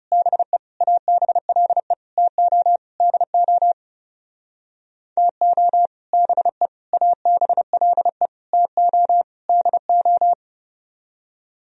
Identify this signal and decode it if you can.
Morse code, heard as a keyed tone, about 38 wpm, 700 Hz. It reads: BE ABLE TO DO TO BE ABLE TO DO